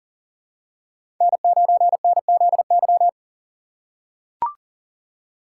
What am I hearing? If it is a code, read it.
Morse code, heard as a keyed tone, about 40 wpm, 700 Hz. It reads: N9NZY
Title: USA Call Signs - 5 Characters - 4x Character Spacing - 40wpm